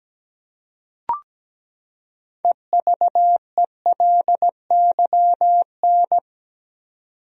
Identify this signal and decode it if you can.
Morse code, heard as a keyed tone, about 17 wpm, 700 Hz. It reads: EVELYN
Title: USA Common Names - 17wpm